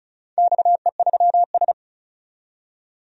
Morse code, heard as a keyed tone, about 36 wpm, 700 Hz. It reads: XE3S
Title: DXCC-100 Call Signs - 4 Characters - 35wpm